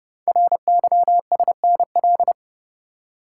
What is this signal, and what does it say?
Morse code, heard as a keyed tone, about 29 wpm, 700 Hz. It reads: RYSNL
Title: Code Groups - 5 Letters - 30wpm